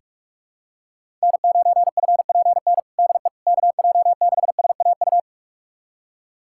Morse code, heard as a keyed tone, about 43 wpm, 700 Hz. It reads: N9FPN DE KJ6SAU
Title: USA Callsign DE Callsign 45wpm